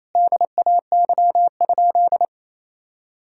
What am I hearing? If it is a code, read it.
Morse code, heard as a keyed tone, about 28 wpm, 700 Hz. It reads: DAY?
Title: Sentences from Top 1000 Words - Review Part 5 - 28wpm